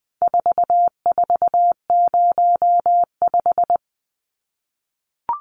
Morse code, heard as a keyed tone, about 20 wpm, 700 Hz. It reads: 4405
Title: Numbers - 4-Digit 20wpm